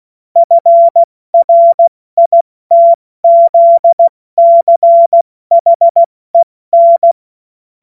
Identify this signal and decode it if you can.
Morse code, heard as a keyed tone, about 16 wpm, 680 Hz. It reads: FRITZCHEN
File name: Morsekurs / Witze / Witz 03. August 2021, 70 F80